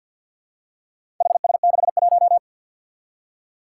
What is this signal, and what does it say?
Morse code, heard as a keyed tone, about 51 wpm, 700 Hz. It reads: HS61
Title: Four Challenging Letter-Number Combinations 50wpm